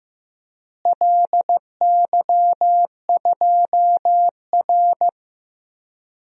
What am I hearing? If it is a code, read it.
Morse code, heard as a keyed tone, about 15 wpm, 700 Hz. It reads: LY2R
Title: DXCC-100 Call Signs - 4 Characters - Repeated 2x - 15wpm